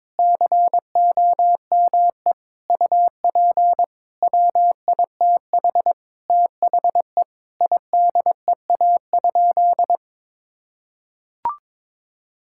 Morse code, heard as a keyed tone, about 22 wpm, 700 Hz. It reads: COME UP WITH THE IDEA?
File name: practice / 22wpm / Sentences with 7-Word Phrases - 22wpm